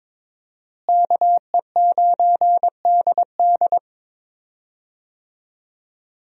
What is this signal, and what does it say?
Morse code, heard as a keyed tone, about 22 wpm, 700 Hz. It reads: KE9DD
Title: USA Call Signs - 5 Characters - Repeated 3x - 22wpm